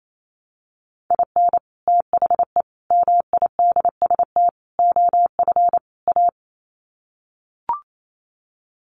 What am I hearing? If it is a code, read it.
Morse code, heard as a keyed tone, about 28 wpm, 700 Hz. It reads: IN THE MIDST OF A
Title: Sets of 5 Words 28wpm